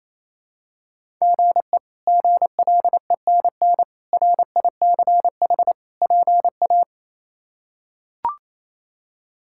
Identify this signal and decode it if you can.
Morse code, heard as a keyed tone, about 28 wpm, 700 Hz. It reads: GE GLENN RICH PA
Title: K1USN SST QSO Practice - 28wpm